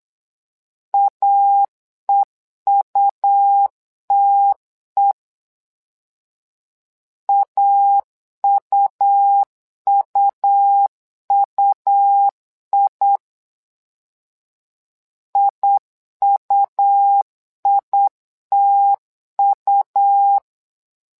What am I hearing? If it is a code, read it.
Morse code, heard as a keyed tone, about 8 wpm, 800 Hz. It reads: AEUTE AUUUI IUITU
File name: LEZIONI / 2 UHDN / 2 U 8wpm 2028